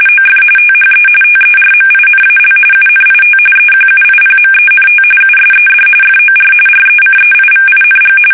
RTTY, about 45.45 baud. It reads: OF THE TERMINAL UNIT EQUIPMENT USED FOR RECEIVING
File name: fsk850 / teletype-RTTY